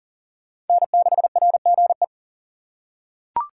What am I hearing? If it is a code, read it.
Morse code, heard as a keyed tone, about 37 wpm, 700 Hz. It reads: N6RGE
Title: USA Call Signs - 5 Characters - 3x Character Spacing - 40wpm